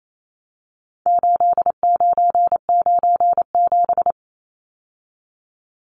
Morse code, heard as a keyed tone, about 28 wpm, 700 Hz. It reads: 8997
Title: Numbers - 4-Digit 28wpm